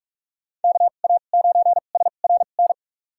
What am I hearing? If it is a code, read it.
Morse code, heard as a keyed tone, about 46 wpm, spaced as for 32 wpm, 700 Hz. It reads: KA9SRN
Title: USA Call Signs - 6 Characters - 2x Character Spacing - 45wpm